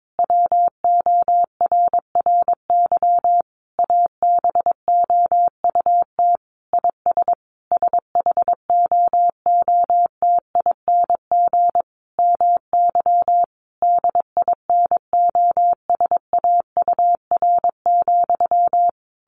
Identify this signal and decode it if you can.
Morse code, heard as a keyed tone, about 22 wpm, 700 Hz. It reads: WORRY ABOUT IS SHOOTING MY DINOSAUR,
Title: A Sound Of Thunder 22wpm